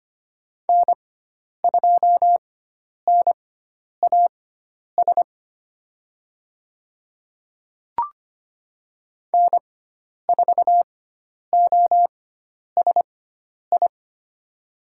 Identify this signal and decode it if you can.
Morse code, heard as a keyed tone, about 25 wpm, spaced as for 10 wpm, 700 Hz. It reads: N2NAS N4OSI
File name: Lesson 16g - Letter c - Call Signs 25wpm